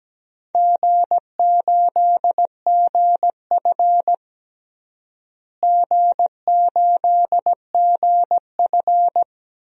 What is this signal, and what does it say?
Morse code, heard as a keyed tone, about 17 wpm, 700 Hz. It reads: G8GF G8GF
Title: DXCC-100 Call Signs - 4 Characters - Repeated 2x - 17wpm